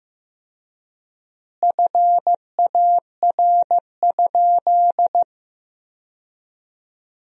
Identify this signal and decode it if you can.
Morse code, heard as a keyed tone, about 15 wpm, 700 Hz. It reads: FAR?
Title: 4x Word-Spacing - Sentences from Top 300 Words - 15wpm